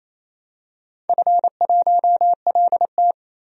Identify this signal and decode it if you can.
Morse code, heard as a keyed tone, about 27 wpm, 700 Hz. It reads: F1LT
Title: DXCC-100 Call Signs - 4 Characters - 28wpm